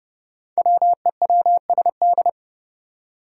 Morse code, heard as a keyed tone, about 30 wpm, 700 Hz. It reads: WEWSD
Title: Five Challenging Letter-Number Combinations 30wpm